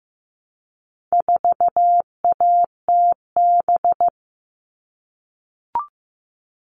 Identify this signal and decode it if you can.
Morse code, heard as a keyed tone, about 15 wpm, 700 Hz. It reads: 4ATB
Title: All Cut Numbers - 4-Digit 15wpm